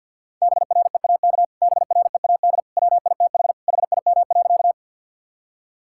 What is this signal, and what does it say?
Morse code, heard as a keyed tone, about 50 wpm, 700 Hz. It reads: BREAK BREAD WITH HIM.
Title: Sentences from Top 1600 Words 50wpm